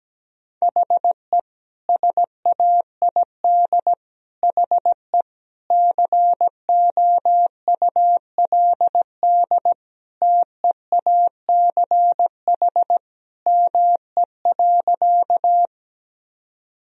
Morse code, heard as a keyed tone, about 17 wpm, 700 Hz. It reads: HE SAID HE COULD TEACH ME.
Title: Sentences from Top 1000 Words - Review Part 2 - 17wpm